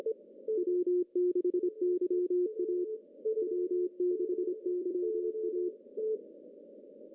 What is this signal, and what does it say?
Morse code, heard as a keyed tone, about 26 wpm, 360 Hz. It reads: W6YA W6YA
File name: CW / cqww-cw-k5tr